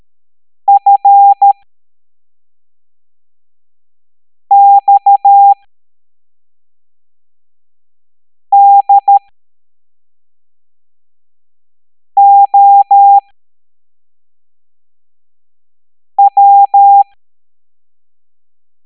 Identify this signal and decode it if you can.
Morse code, heard as a keyed tone, about 13 wpm, spaced as for 3 wpm, 800 Hz. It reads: FXDOW